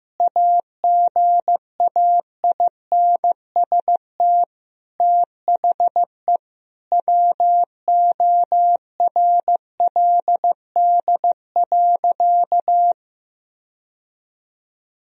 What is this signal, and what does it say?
Morse code, heard as a keyed tone, about 16 wpm, 700 Hz. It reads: AGAINST THE WORLD.
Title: Sentences from Top 1000 Words - Review Part 2 - 15wpm